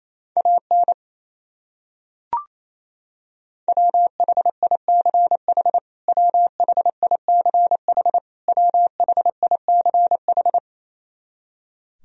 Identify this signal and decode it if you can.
Morse code, heard as a keyed tone, about 28 wpm, 700 Hz. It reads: AN WHICH WHICH WHICH
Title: Top 100 Words Repeated 28wpm